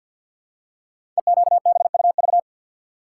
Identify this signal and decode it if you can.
Morse code, heard as a keyed tone, about 46 wpm, 700 Hz. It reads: EQBUV
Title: Code Groups - 5 Letters - 50wpm